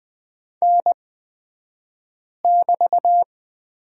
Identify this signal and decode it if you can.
Morse code, heard as a keyed tone, about 20 wpm, spaced as for 5 wpm, 700 Hz. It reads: N<BT>
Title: 223 Lesson 36d - BT - Two Characters 20wpm